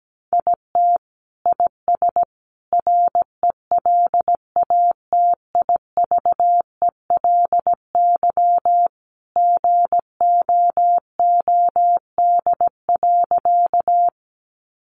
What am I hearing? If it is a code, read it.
Morse code, heard as a keyed tone, about 17 wpm, 700 Hz. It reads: IT IS RELATIVELY GOOD.
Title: Sentences from Top 1100 Words 17wpm